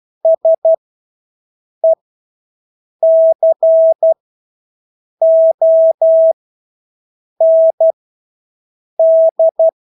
Morse code, heard as a keyed tone, about 12 wpm, spaced as for 6 wpm, 650 Hz. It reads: SECOND
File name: Engelske ord 12-6 wpm